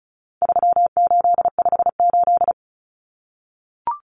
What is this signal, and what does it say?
Morse code, heard as a keyed tone, about 34 wpm, 700 Hz. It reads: 3858
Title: Common Cut Numbers - 4-Digit 35wpm